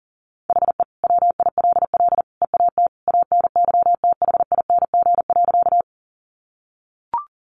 Morse code, heard as a keyed tone, about 40 wpm, 700 Hz. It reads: HE WILL EAT ANYTHING.
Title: Sentences from Top 1000 Words 40wpm